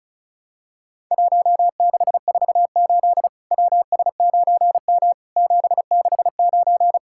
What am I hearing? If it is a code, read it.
Morse code, heard as a keyed tone, about 36 wpm, 700 Hz. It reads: 1648 WS9M 769
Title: Worked All Europe DX Contest - QTC Exchange - 35wpm